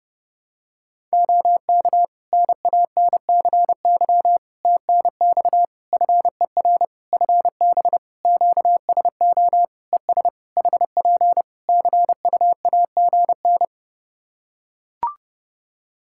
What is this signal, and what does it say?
Morse code, heard as a keyed tone, about 30 wpm, 700 Hz. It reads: OK NANCY TNX FER FB QSO ES HP CUAGN
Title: Practice QSO - Exchange 4 - Ending - 30wpm